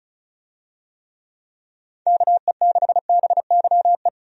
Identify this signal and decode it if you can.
Morse code, heard as a keyed tone, about 34 wpm, 700 Hz. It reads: KE6BYE
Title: USA Call Signs - 6 Characters - 35wpm